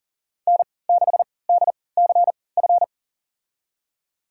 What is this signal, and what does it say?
Morse code, heard as a keyed tone, about 37 wpm, spaced as for 22 wpm, 700 Hz. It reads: N6DCF
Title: USA Call Signs - 5 Characters - 3x Character Spacing - 40wpm